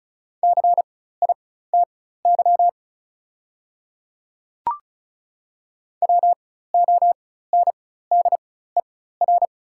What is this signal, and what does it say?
Morse code, heard as a keyed tone, about 36 wpm, spaced as for 16 wpm, 700 Hz. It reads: CITY WONDER
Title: 4x Character Spacing - Top 500 Words - 35wpm